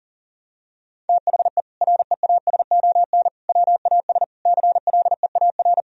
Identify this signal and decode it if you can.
Morse code, heard as a keyed tone, about 40 wpm, 700 Hz. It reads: THE REASON WAS CLEAR
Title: Sentences from Top 1000 Words - Review Part 2 - 40wpm